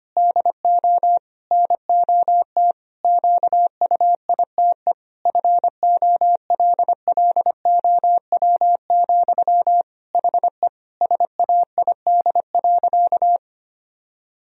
Morse code, heard as a keyed tone, about 25 wpm, 700 Hz. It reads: DO NOT QUITE FOLLOW, HE SAID.
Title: Sentences from Top 1000 Words - Review Part 1 - 25wpm